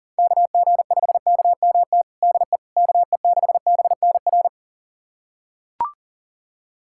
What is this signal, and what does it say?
Morse code, heard as a keyed tone, about 37 wpm, 700 Hz. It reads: KG5KMT DE KE6BNR
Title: USA Callsign DE Callsign - 6 Characters - 40wpm